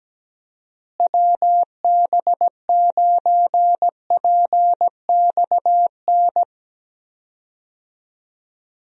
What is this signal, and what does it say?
Morse code, heard as a keyed tone, about 17 wpm, 700 Hz. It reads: WB9PXN
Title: USA Call Signs - 6 Characters - Repeated 3x - 17wpm